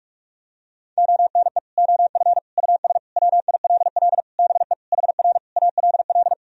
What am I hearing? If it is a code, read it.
Morse code, heard as a keyed tone, about 46 wpm, 700 Hz. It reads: ONE OF US WILL BE HR ALL
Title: Sentences from Top 300 Words and Abbreviations 45wpm